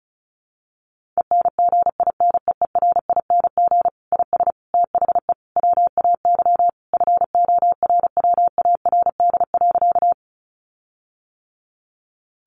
Morse code, heard as a keyed tone, about 34 wpm, 700 Hz. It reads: ENGINEERING IS THE WAY FORWARD.
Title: Sentences from Top 1500 Words 35wpm